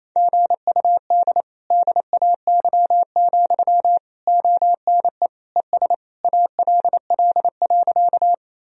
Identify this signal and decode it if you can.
Morse code, heard as a keyed tone, about 28 wpm, 700 Hz. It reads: GUD DAY, ONE ES ALL.
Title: Sentences from Top 200 Words and Abbreviations 28wpm